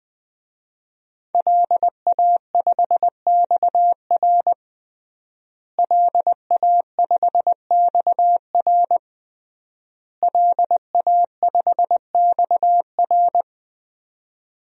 Morse code, heard as a keyed tone, about 21 wpm, 700 Hz. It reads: LA5XR LA5XR LA5XR
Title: DXCC-100 Call Signs - 5 Characters - Repeated 3x - 20wpm